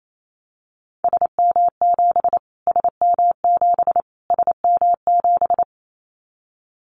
Morse code, heard as a keyed tone, about 28 wpm, 700 Hz. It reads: SM7 SM7 SM7
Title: Callsign Prefixes for Top 25 DXCC - Repeated 3x - 28wpm